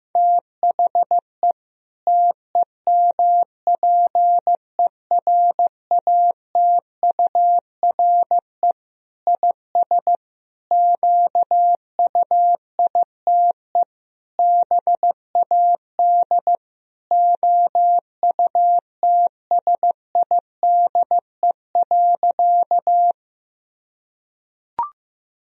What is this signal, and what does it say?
Morse code, heard as a keyed tone, about 15 wpm, 700 Hz. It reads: THE TEMPERATURE IS QUITE BAD OUTSIDE.
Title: Sentences from Top 1000 Words - Review Part 4 - 15wpm